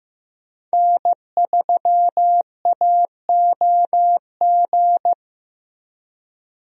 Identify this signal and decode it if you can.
Morse code, heard as a keyed tone, about 15 wpm, 700 Hz. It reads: N3AOG
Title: USA Call Signs - 5 Characters - Repeated 2x - 15wpm